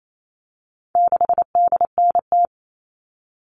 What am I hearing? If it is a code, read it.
Morse code, heard as a keyed tone, about 28 wpm, 700 Hz. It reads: 6DNT
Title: All Cut Numbers - 4-Digit 28wpm